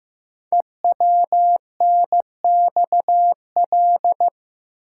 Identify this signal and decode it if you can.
Morse code, heard as a keyed tone, about 15 wpm, 700 Hz. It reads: EWNXL
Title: Code Groups - 5 Letters - 15wpm